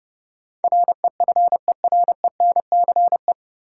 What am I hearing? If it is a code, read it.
Morse code, heard as a keyed tone, about 30 wpm, 700 Hz. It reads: REFERENCE
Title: Sentences from Top 1200 Words - New Word Drill Alt - 30wpm